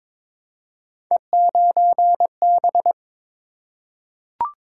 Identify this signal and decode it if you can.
Morse code, heard as a keyed tone, about 22 wpm, 700 Hz. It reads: E9B
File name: practice / 22wpm / Three Letter-Number Combinations - 22wpm